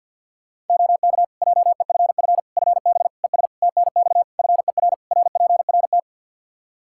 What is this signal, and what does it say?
Morse code, heard as a keyed tone, about 52 wpm, 700 Hz. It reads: OK JEFF FB ES TNX FER RPRT
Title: K1USN SST QSO Practice 50wpm